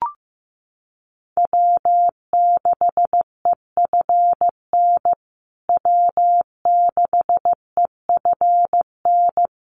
Morse code, heard as a keyed tone, about 15 wpm, 700 Hz. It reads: W6EFN W6EFN
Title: USA Call Signs Repeated 3x 15wpm